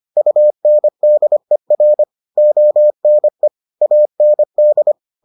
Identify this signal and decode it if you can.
Morse code, heard as a keyed tone, about 25 wpm, 600 Hz. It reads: UNDER ONE AND